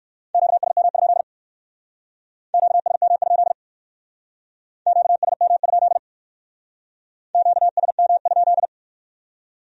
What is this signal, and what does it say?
Morse code, heard as a keyed tone, about 56 wpm, 700 Hz. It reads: QSM? QSM? QSM? QSM?
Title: Speed-Racing Q-Codes - 45wpm